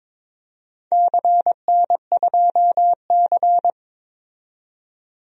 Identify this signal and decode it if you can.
Morse code, heard as a keyed tone, about 21 wpm, 700 Hz. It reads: CN2C
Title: DXCC-100 Call Signs - 4 Characters - 22wpm